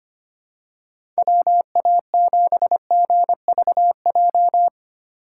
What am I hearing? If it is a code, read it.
Morse code, heard as a keyed tone, about 25 wpm, 700 Hz. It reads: WA7GVJ
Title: USA Call Signs - 6 Characters - 2x Character Spacing - 25wpm